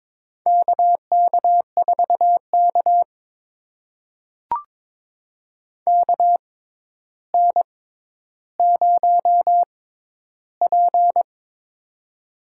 Morse code, heard as a keyed tone, about 22 wpm, spaced as for 11 wpm, 700 Hz. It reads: KK4K KN0P
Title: USA Call Signs - 4 Characters - 6x Character Spacing - 22wpm